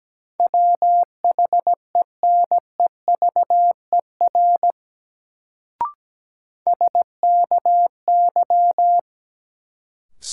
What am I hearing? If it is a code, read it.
Morse code, heard as a keyed tone, about 17 wpm, 700 Hz. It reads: WHENEVER SKY
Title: Sentences from Top 1200 Words - New Word Drill Alt - 17wpm